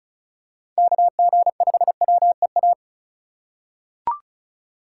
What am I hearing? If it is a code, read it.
Morse code, heard as a keyed tone, about 36 wpm, 700 Hz. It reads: KG5WEA
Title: USA Call Signs - 6 Characters - 3x Character Spacing - 35wpm